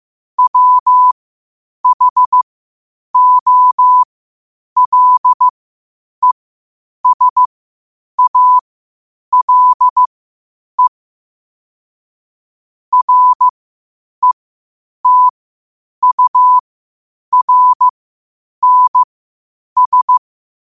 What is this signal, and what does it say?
Morse code, heard as a keyed tone, about 15 wpm, spaced as for 8 wpm, 1000 Hz. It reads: WHOLESALE RETURNS
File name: qotdfiles / 8wpm